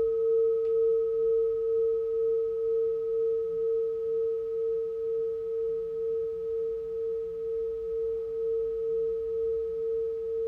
Dial tone (China, 450 Hz).